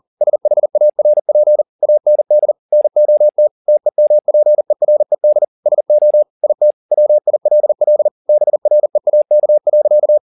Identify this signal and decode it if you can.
Morse code, heard as a keyed tone, about 40 wpm, 600 Hz. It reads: SHARP AND NOT TEMPERED SO IT WILL BREAK.
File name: the old man and the sea 2 40wpm